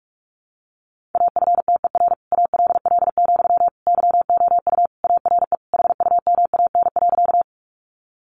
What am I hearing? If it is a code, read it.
Morse code, heard as a keyed tone, about 44 wpm, 700 Hz. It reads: AFTER ALL, YOU ARE HUMAN.
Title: Sentences from Top 600 Words 45wpm